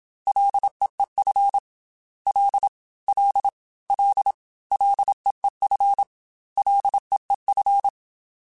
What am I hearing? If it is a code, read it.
Morse code, heard as a keyed tone, about 27 wpm, 800 Hz. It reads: LEEF L L L LEEF LEEF
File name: Ls / Single L m3s / Single L @25